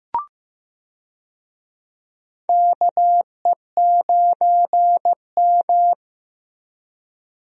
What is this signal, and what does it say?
Morse code, heard as a keyed tone, about 16 wpm, 700 Hz. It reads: KE9M
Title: USA Call Signs - 4 Characters - Repeated 3x - 15wpm